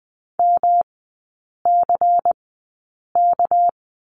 Morse code, heard as a keyed tone, about 20 wpm, spaced as for 8 wpm, 700 Hz. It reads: MCK
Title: 198 Lesson 32e - Letter k - Three Characters 20wpm